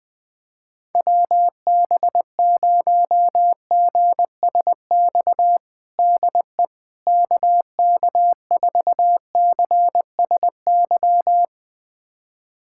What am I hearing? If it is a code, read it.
Morse code, heard as a keyed tone, about 20 wpm, 700 Hz. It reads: WB0GSX DE KK4CSY
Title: USA Callsign DE Callsign - 6 Characters - 20wpm